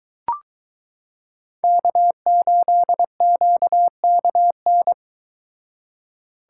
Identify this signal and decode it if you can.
Morse code, heard as a keyed tone, about 23 wpm, 700 Hz. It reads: K8QKN